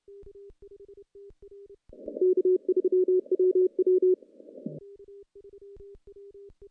Morse code, heard as a keyed tone, about 26 wpm, 370 Hz. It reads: K3WW
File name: contests / CW-SS-k5tr- -1300